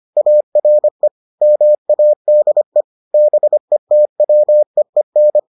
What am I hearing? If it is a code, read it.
Morse code, heard as a keyed tone, about 25 wpm, 600 Hz. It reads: ARE MADE BETWEEN